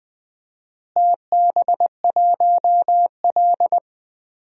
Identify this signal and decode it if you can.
Morse code, heard as a keyed tone, about 20 wpm, 700 Hz. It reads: TB1L
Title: DXCC-100 Call Signs - 4 Characters - Repeated 2x - 20wpm